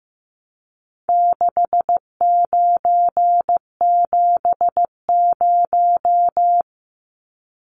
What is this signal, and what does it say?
Morse code, heard as a keyed tone, about 15 wpm, 700 Hz. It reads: 6970
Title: Numbers - 4-Digit 15wpm